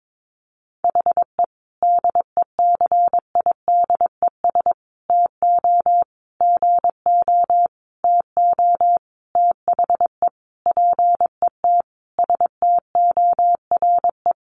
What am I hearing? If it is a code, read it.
Morse code, heard as a keyed tone, about 22 wpm, 700 Hz. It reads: HE DECIDES TO GO TO THE PET STORE
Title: Additional Easy Phrases 22wpm